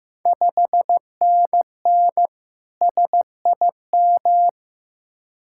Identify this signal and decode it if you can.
Morse code, heard as a keyed tone, about 16 wpm, 700 Hz. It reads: 5NN SIM
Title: Ontario QSO Party Exchange 15wpm